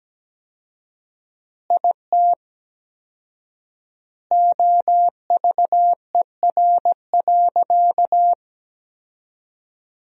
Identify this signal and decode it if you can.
Morse code, heard as a keyed tone, about 17 wpm, 700 Hz. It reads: IT OVER.
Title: 4x Word-Spacing - Sentences from Top 200 Words - 17wpm